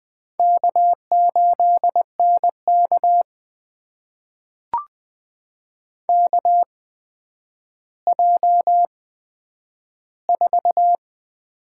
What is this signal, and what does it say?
Morse code, heard as a keyed tone, about 20 wpm, spaced as for 10 wpm, 700 Hz. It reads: K8NK KJ4
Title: USA Call Signs - 4 Characters - 8x Character Spacing - 20wpm